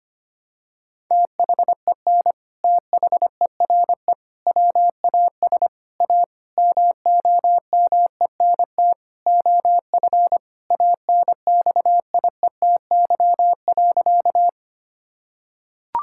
Morse code, heard as a keyed tone, about 25 wpm, 700 Hz. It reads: THEN THERE WAS A MOMENT OF ANXIETY.
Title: Sentences from Top 1600 Words 25wpm